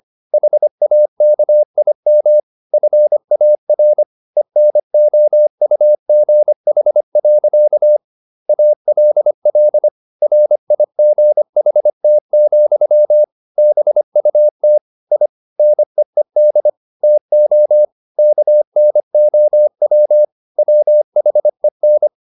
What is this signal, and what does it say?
Morse code, heard as a keyed tone, about 25 wpm, 600 Hz. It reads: HAKIM FAR ENOUGH. ALL RIGHT, BUT I NEED TO KNOW WHEN